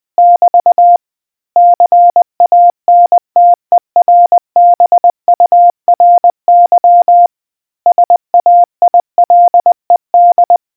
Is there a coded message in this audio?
Yes: Morse code from 0.2 to 10.6 s, heard as a keyed tone, about 20 wpm, 700 Hz. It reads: <BT> CANTERBURY SAILED